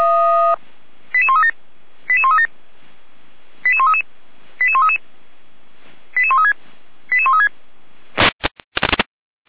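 Five-tone selective call (ZVEI), heard as quick runs of five tones, tones of about 70 ms. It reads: 88137 88137 88133 88133 88136 88136
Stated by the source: ZVEI five-tone call received by SDR.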